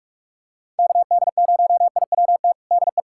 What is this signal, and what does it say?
Morse code, heard as a keyed tone, about 46 wpm, 700 Hz. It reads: KD0IWT DE
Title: USA Callsign DE Callsign - 6 Characters - 45wpm